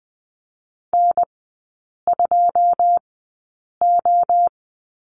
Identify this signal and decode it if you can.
Morse code, heard as a keyed tone, about 20 wpm, spaced as for 8 wpm, 700 Hz. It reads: N2O
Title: 127 Lesson 21e - Number 6 - Three Characters 20wpm